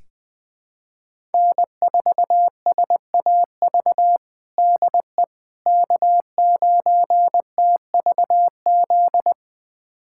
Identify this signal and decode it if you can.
Morse code, heard as a keyed tone, about 20 wpm, 700 Hz. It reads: N4SAV DE K9TVZ
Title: USA Callsign DE Callsign - 5 Characters - 20wpm